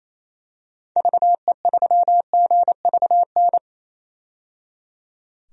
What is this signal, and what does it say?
Morse code, heard as a keyed tone, about 28 wpm, 700 Hz. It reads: VE3GVN
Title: DXCC-100 Call Signs - 6 Characters - 28wpm